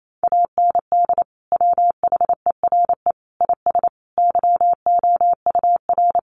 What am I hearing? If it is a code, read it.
Morse code, heard as a keyed tone, about 28 wpm, 700 Hz. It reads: AND WHERE IS YOUR
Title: Sentences from Top 1300 Words 28wpm